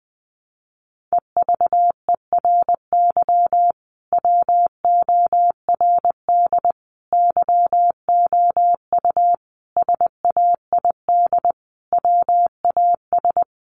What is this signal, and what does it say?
Morse code, heard as a keyed tone, about 20 wpm, 700 Hz. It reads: EVERY WORD YOU SAID WAS
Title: Sentences from Top 400 Words 20wpm